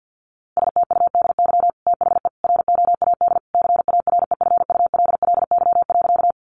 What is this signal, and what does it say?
Morse code, heard as a keyed tone, about 50 wpm, 700 Hz. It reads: STUDY THE ROAD CAREFULLY.
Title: Sentences from Top 1300 Words 50wpm